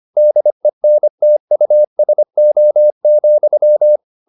Morse code, heard as a keyed tone, about 25 wpm, 600 Hz. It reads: DENTUSO,